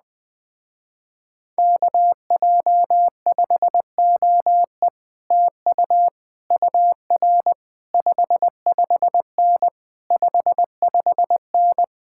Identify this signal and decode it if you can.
Morse code, heard as a keyed tone, about 20 wpm, 700 Hz. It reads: KJ5OE TU UR 55N 55N
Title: Speed-Racing Common QSO Phrases Encore - 17wpm